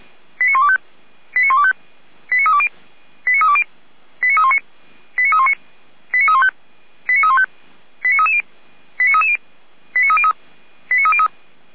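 Five-tone selective call (ZVEI), heard as quick runs of five tones, tones of about 70 ms. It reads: 89125 89125 89320 89320 89319 89319 89315 89315 89330 89330 89303 89303
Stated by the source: ZVEI five-tone selective call received by SDR.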